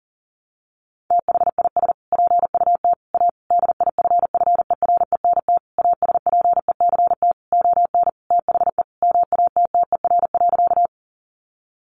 Morse code, heard as a keyed tone, about 37 wpm, 700 Hz. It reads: THIS PUT A DIFFERENT ASPECT ON THE MATTER.
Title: Sentences from Top 1300 Words 40wpm